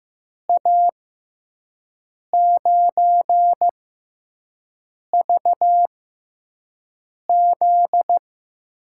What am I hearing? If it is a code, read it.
Morse code, heard as a keyed tone, about 15 wpm, spaced as for 5 wpm, 700 Hz. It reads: A9VZ
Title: USA Call Signs - 5 Characters - 6x Character Spacing - 15wpm